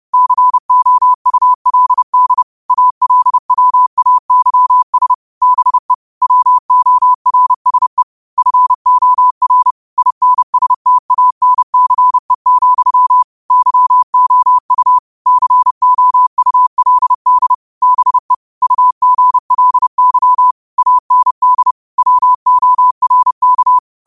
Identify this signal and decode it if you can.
Morse code, heard as a keyed tone, about 31 wpm, 1000 Hz. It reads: COULD ALWAYS BE WORSE FOR INSTANCE, YOU COULD BE UGLY AND WORK
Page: New quotes every day in morse code at 30 Words per minute.